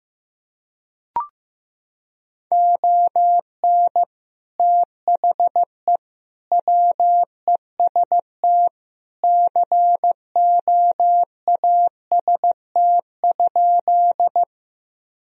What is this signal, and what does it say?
Morse code, heard as a keyed tone, about 15 wpm, 700 Hz. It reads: ON THE WEST COAST?
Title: Sentences from Top 1000 Words - Review Part 4 - 15wpm